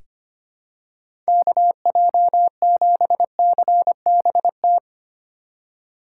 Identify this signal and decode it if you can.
Morse code, heard as a keyed tone, about 25 wpm, 700 Hz. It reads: KJ7CBT
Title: USA Call Signs - 6 Characters - 3x Character Spacing - 25wpm